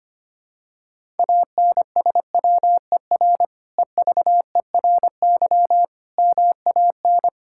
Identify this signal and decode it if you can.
Morse code, heard as a keyed tone, about 25 wpm, 700 Hz. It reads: ANSWER EVERY MAN
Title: Sentences from Top 1000 Words - Review Part 5 - 25wpm